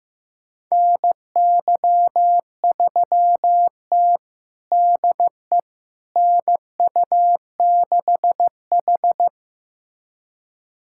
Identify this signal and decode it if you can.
Morse code, heard as a keyed tone, about 15 wpm, 700 Hz. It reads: NY3T DE NU6H
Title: USA Callsign DE Callsign - 4 Characters - 15wpm